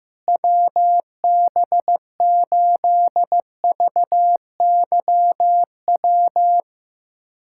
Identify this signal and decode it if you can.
Morse code, heard as a keyed tone, about 15 wpm, 700 Hz. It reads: WB8VYW
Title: USA Callsign DE Callsign 15wpm